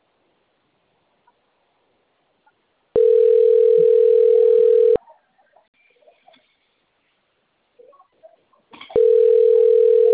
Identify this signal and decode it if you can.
Ringback tone (North America), 440 + 480 Hz.